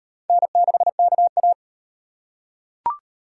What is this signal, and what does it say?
Morse code, heard as a keyed tone, about 38 wpm, 700 Hz. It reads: N6KA